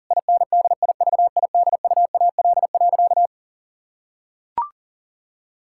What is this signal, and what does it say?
Morse code, heard as a keyed tone, about 40 wpm, 700 Hz. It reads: INDIVIDUAL.
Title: Sentences from Top 1000 Words - Review Part 6 - 40wpm